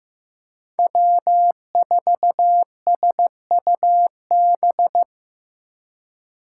Morse code, heard as a keyed tone, about 15 wpm, 700 Hz. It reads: W4SUB
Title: USA Call Signs - 4x Character Spacing - 15wpm